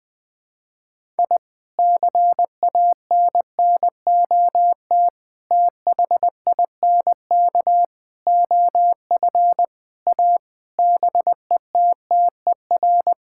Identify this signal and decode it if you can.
Morse code, heard as a keyed tone, about 21 wpm, 700 Hz. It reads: I CANNOT THINK OF A BETTER
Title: Sentences with 5-Word Phrases - Spoken Beginning - 20wpm